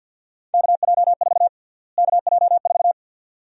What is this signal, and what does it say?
Morse code, heard as a keyed tone, about 52 wpm, 700 Hz. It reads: KJ4 KJ4
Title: USA Call Sign Prefixes - Repeated 2x - 50wpm